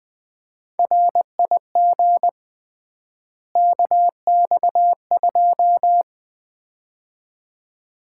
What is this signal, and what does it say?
Morse code, heard as a keyed tone, about 20 wpm, 700 Hz. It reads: RIG KX2
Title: Common QSO Phrases - 3x Word-Spacing - 20wpm